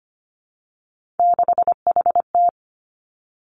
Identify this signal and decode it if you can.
Morse code, heard as a keyed tone, about 25 wpm, 700 Hz. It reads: 6HT
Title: Three Challenging Letter-Number Combinations 25wpm